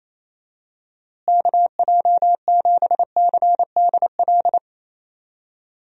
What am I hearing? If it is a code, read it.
Morse code, heard as a keyed tone, about 28 wpm, 700 Hz. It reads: KJ7CDL
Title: USA Call Signs - 6 Characters - Repeated 3x - 28wpm